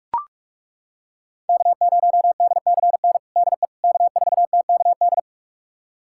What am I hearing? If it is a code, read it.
Morse code, heard as a keyed tone, about 42 wpm, 700 Hz. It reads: K0DCN DE K4TKD
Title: USA Callsign DE Callsign - 5 Characters - 45wpm